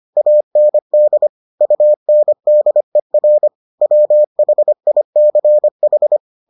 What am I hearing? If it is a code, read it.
Morse code, heard as a keyed tone, about 25 wpm, 600 Hz. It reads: AND UNDER WHICH